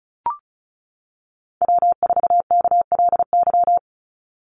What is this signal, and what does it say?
Morse code, heard as a keyed tone, about 36 wpm, 700 Hz. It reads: W4KLY
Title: Call Signs - 35wpm